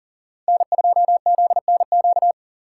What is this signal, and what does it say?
Morse code, heard as a keyed tone, about 52 wpm, 700 Hz. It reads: N2ZNQ